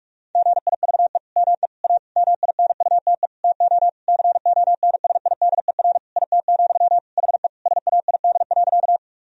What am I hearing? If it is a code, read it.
Morse code, heard as a keyed tone, about 41 wpm, 700 Hz. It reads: GIVE ME A MINUTE TO CONSIDER IT, HE SAID.
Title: Sentences from Top 1000 Words - Review Part 4 - 45wpm